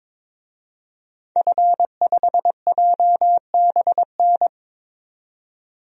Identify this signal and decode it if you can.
Morse code, heard as a keyed tone, about 21 wpm, 700 Hz. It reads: F5JBN
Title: DXCC-100 Call Signs - 5 Characters - 22wpm